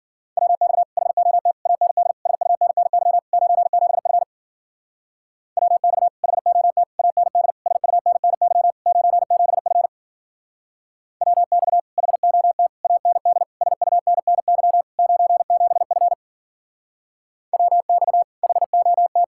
Morse code, heard as a keyed tone, about 52 wpm, 700 Hz. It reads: WX HOT AND SUNNY 97F WX HOT AND SUNNY 97F WX HOT AND SUNNY 97F WX HOT